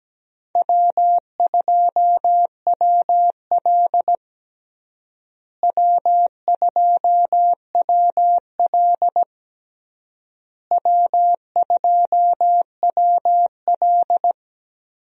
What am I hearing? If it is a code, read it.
Morse code, heard as a keyed tone, about 17 wpm, 700 Hz. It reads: W2WL W2WL W2WL
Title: USA Call Signs - 4 Characters - Repeated 3x - 17wpm